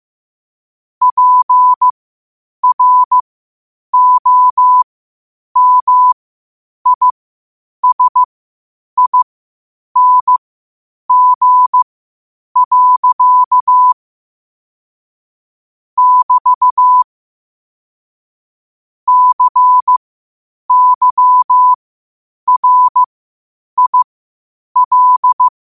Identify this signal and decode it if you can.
Morse code, heard as a keyed tone, about 15 wpm, spaced as for 8 wpm, 1000 Hz. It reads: PROMISING. <BT> CYRIL
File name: qotdfiles / 8wpm